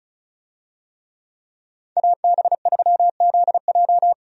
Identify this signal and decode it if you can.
Morse code, heard as a keyed tone, about 36 wpm, 700 Hz. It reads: AB3ZJ
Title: USA Call Signs - 5 Characters - 35wpm